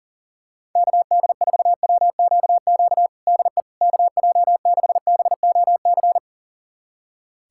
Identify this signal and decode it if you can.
Morse code, heard as a keyed tone, about 40 wpm, 690 Hz. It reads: KD4WQQ DE KJ6BOC
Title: USA Callsign DE Callsign - 6 Characters - 40wpm